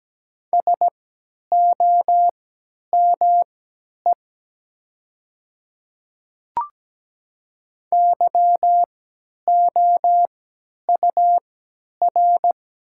Morse code, heard as a keyed tone, about 17 wpm, spaced as for 10 wpm, 700 Hz. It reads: SOME YOUR
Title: 3x Character Spacing - Top 100 Words - 17wpm